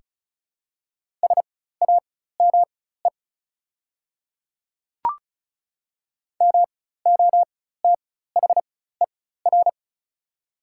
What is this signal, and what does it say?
Morse code, heard as a keyed tone, about 34 wpm, spaced as for 16 wpm, 700 Hz. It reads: SAME MOTHER
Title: 4x Character Spacing - Top 300 Words - 35wpm